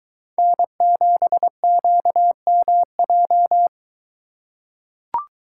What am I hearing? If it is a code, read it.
Morse code, heard as a keyed tone, about 22 wpm, 700 Hz. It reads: N7QMJ